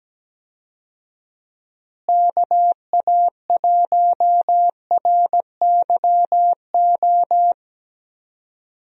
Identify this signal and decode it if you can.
Morse code, heard as a keyed tone, about 17 wpm, 700 Hz. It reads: KA1RYO
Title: USA Call Signs - 6 Characters - Repeated 3x - 17wpm